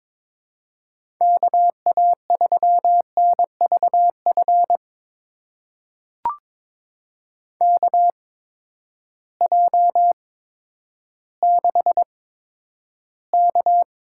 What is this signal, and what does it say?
Morse code, heard as a keyed tone, about 22 wpm, spaced as for 11 wpm, 700 Hz. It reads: KA3NVF KJ6K
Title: USA Call Signs - 6 Characters - 8x Character Spacing - 22wpm